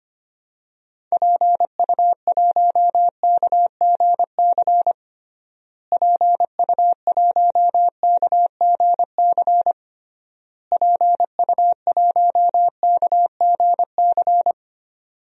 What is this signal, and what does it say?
Morse code, heard as a keyed tone, about 25 wpm, 700 Hz. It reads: PU1KGC PU1KGC PU1KGC
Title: DXCC-100 Call Signs - 6 Characters - Repeated 3x - 25wpm